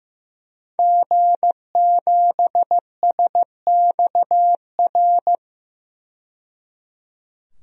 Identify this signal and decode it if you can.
Morse code, heard as a keyed tone, about 16 wpm, 700 Hz. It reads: G7SXR
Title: DXCC-100 Call Signs - 5 Characters - 15wpm